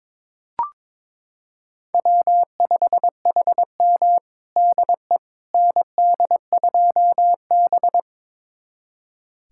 Morse code, heard as a keyed tone, about 21 wpm, 700 Hz. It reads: W5HM DE ND2B
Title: USA Callsign DE Callsign - 4 Characters - 22wpm